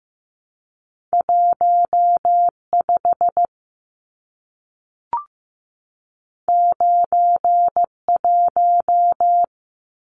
Morse code, heard as a keyed tone, about 15 wpm, 700 Hz. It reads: 15 91
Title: Numbers - 2-Digit 15wpm